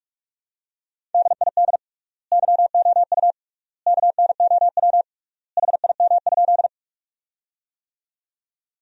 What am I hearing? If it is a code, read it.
Morse code, heard as a keyed tone, about 41 wpm, 700 Hz. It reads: DID YOU KNOW HIM?
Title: 3x Word-Spacing - Sentences from Top 100 Words - 45wpm